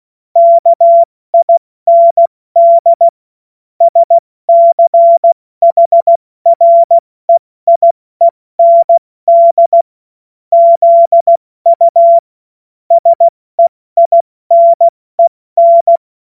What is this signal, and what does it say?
Morse code, heard as a keyed tone, about 16 wpm, 680 Hz. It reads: KIND SCHREIEND ZU SEINEN
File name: Morsekurs / Witze / Witz 16. November 2021, 70 F80